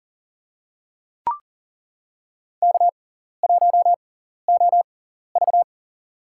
Morse code, heard as a keyed tone, about 39 wpm, spaced as for 14 wpm, 700 Hz. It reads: K1OV
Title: USA Call Signs - 4 Characters - 6x Character Spacing - 40wpm